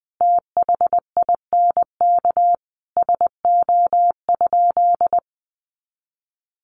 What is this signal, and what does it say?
Morse code, heard as a keyed tone, about 20 wpm, 700 Hz. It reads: THINK SO?
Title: Sentences from Top 200 Words 20wpm